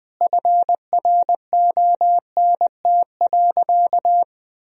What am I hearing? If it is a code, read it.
Morse code, heard as a keyed tone, about 20 wpm, 700 Hz. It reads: FRONT.
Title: Sentences from Top 1000 Words - Review Part 2 - 20wpm